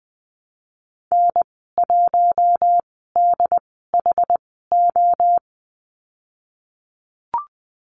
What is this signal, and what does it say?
Morse code, heard as a keyed tone, about 20 wpm, spaced as for 15 wpm, 700 Hz. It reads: N1DHO
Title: 088 Lesson 15g - Number 5 - Call Signs 20wpm